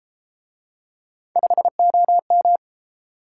Morse code, heard as a keyed tone, about 29 wpm, 700 Hz. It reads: 5OM